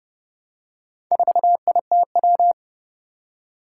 Morse code, heard as a keyed tone, about 30 wpm, 700 Hz. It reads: 4ITW
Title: Four Challenging Letter-Number Combinations 30wpm